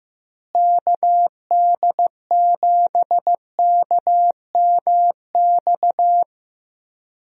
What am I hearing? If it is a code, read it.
Morse code, heard as a keyed tone, about 15 wpm, 700 Hz. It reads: KD7KMX
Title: USA Call Signs - 6 Characters - Repeated 2x - 15wpm